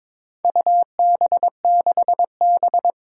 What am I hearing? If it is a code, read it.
Morse code, heard as a keyed tone, about 21 wpm, 700 Hz. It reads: UB6B
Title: Four Challenging Letter-Number Combinations 22wpm